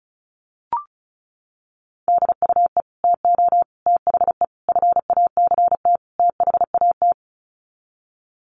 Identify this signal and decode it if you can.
Morse code, heard as a keyed tone, about 34 wpm, 700 Hz. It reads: DUE TO THE FACT THAT
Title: Sets of 5 Words Encore 35wpm